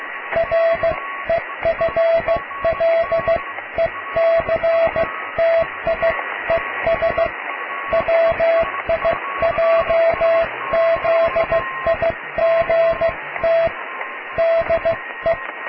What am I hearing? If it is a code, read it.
Morse code, heard as a keyed tone, about 15 wpm, 660 Hz, with radio noise. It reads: REFLECTIES WIJZIGT DE
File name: cw / tekst(a)+QRM